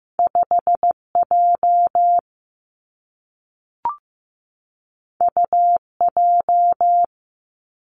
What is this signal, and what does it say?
Morse code, heard as a keyed tone, about 15 wpm, 700 Hz. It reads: 5 J U J
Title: Two Challenging Letter-Number Combinations 15wpm